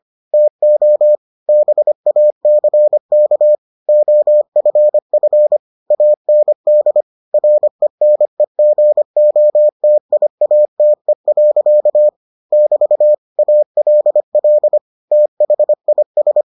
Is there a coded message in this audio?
Yes: Morse code from 0.3 to 16.4 s, heard as a keyed tone, about 25 wpm, 600 Hz. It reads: TO BACK OFF AND RENEGOTIATE. <BT> ALL THIS